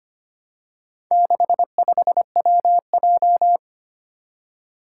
Morse code, heard as a keyed tone, about 25 wpm, 700 Hz. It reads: 65WJ
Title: Four Challenging Letter-Number Combinations 25wpm